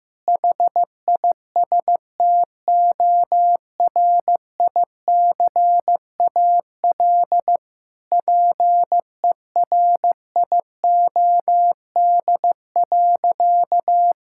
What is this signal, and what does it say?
Morse code, heard as a keyed tone, about 15 wpm, 700 Hz. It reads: HISTORICAL PERIOD.
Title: Sentences from Top 1000 Words - Review Part 5 - 15wpm